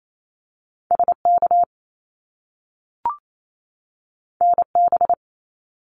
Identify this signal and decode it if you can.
Morse code, heard as a keyed tone, about 28 wpm, 700 Hz. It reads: SK NB
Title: Canadian Provinces and Territories 28wpm